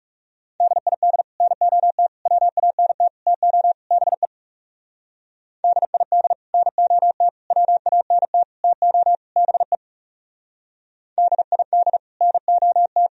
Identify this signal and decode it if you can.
Morse code, heard as a keyed tone, about 38 wpm, 700 Hz. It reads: DID NOT WANT TO BE DID NOT WANT TO BE DID NOT